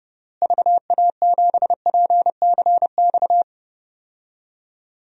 Morse code, heard as a keyed tone, about 30 wpm, 700 Hz. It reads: VA7PCX
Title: DXCC-100 Call Signs - 6 Characters - Repeated 3x - 30wpm